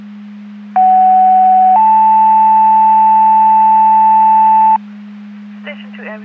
Two-tone page, heard as a one-second tone, then a three-second tone, about 750 then 880 Hz. It is heard from 0.8 to 4.8 s.
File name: Portage 154.385 Paging